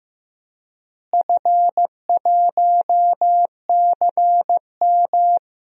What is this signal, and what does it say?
Morse code, heard as a keyed tone, about 15 wpm, 700 Hz. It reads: F1CM
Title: DXCC-100 Call Signs - 4 Characters - Repeated 2x - 15wpm